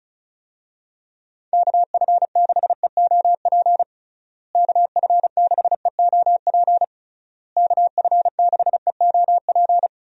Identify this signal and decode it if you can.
Morse code, heard as a keyed tone, about 34 wpm, 700 Hz. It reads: KF6EOP KF6EOP KF6EOP
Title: USA Call Signs - 6 Characters - Repeated 3x - 35wpm